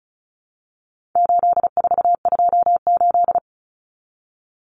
Morse code, heard as a keyed tone, about 34 wpm, 700 Hz. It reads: 8428
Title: Numbers - 4-Digit 35wpm